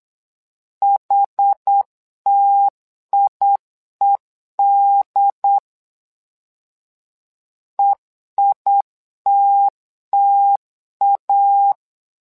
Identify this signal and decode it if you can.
Morse code, heard as a keyed tone, about 8 wpm, 800 Hz. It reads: HTIED EITTA
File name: LEZIONI / 2 UHDN / 2 UHD 8wpm 2258